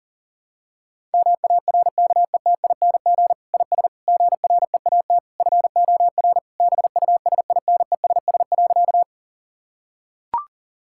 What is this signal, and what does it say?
Morse code, heard as a keyed tone, about 40 wpm, 700 Hz. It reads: MARKETING IS GREAT FOR BUSINESS.
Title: Sentences from Top 1000 Words - Review Part 5 - 40wpm